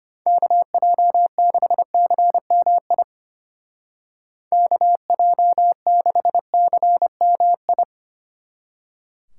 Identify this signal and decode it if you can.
Morse code, heard as a keyed tone, about 30 wpm, 700 Hz. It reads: KJ6CMI KJ6CMI
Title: Speed Racing - USA Call Signs 25wpm